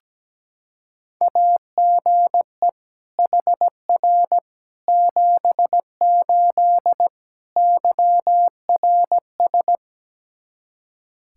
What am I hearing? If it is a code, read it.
Morse code, heard as a keyed tone, about 17 wpm, 700 Hz. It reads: AGE HR 78 YRS
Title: Practice QSO - Exchange 3 - More Info - 17wpm